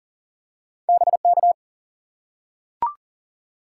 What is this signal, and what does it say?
Morse code, heard as a keyed tone, about 42 wpm, 700 Hz. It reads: BK
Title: Top 180 QSO Elements 40wpm